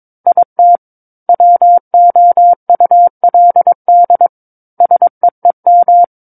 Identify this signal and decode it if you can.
Morse code, heard as a keyed tone, about 22 wpm, 700 Hz. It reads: IT WOULD SEEM
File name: Book One / 13-Chapter 13 - 22 WPM